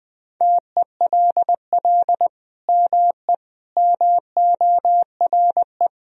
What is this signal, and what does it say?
Morse code, heard as a keyed tone, about 20 wpm, 700 Hz. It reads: TELL ME MORE
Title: Sentences from Top 1000 Words - Review Part 4 - 20wpm